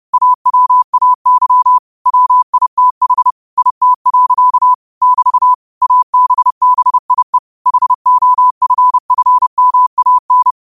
Morse code, heard as a keyed tone, about 30 wpm, 1000 Hz. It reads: AWAY WITH IT. <BT> ABBIE HOFFMAN